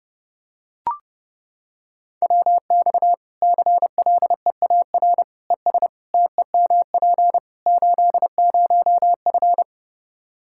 Morse code, heard as a keyed tone, about 30 wpm, 700 Hz. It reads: WX CLEAR ES TEMP 80F
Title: K1USN SST QSO Practice 30wpm